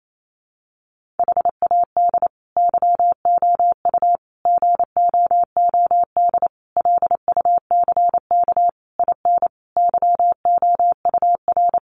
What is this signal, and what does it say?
Morse code, heard as a keyed tone, about 28 wpm, 700 Hz. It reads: HAD YOU GOOD LUCK IN YOUR
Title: Sentences from Top 1500 Words 28wpm